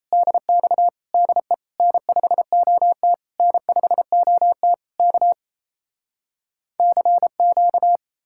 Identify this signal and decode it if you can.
Morse code, heard as a keyed tone, about 37 wpm, 700 Hz. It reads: DX DE N5OT N5OT K CQ